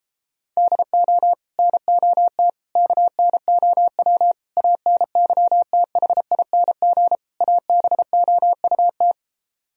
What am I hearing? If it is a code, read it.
Morse code, heard as a keyed tone, about 38 wpm, 700 Hz. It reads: DO NOT KNOW ANYTHING ABOUT